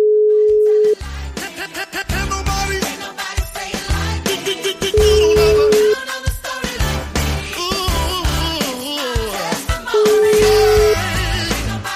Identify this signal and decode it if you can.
Ringback tone (Europe, 425 Hz), with noise.